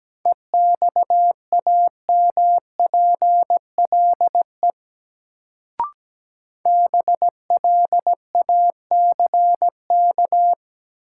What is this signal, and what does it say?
Morse code, heard as a keyed tone, about 17 wpm, 700 Hz. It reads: EXAMPLE BLACK
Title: Sentences from Top 400 Words - New Word Drill - 17wpm